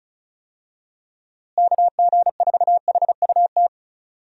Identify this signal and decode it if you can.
Morse code, heard as a keyed tone, about 34 wpm, 700 Hz. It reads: KG4HUT
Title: USA Call Signs - 6 Characters - Repeated 3x - 35wpm